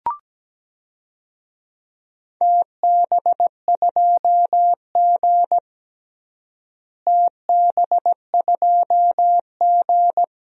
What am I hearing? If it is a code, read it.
Morse code, heard as a keyed tone, about 17 wpm, 700 Hz. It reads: TB2G TB2G
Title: DXCC-100 Call Signs - 4 Characters - Repeated 2x - 17wpm